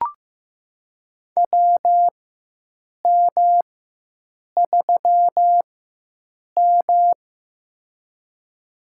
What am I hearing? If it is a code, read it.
Morse code, heard as a keyed tone, about 15 wpm, spaced as for 7 wpm, 700 Hz. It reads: WM3M
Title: USA Call Signs - 4x Character Spacing - 15wpm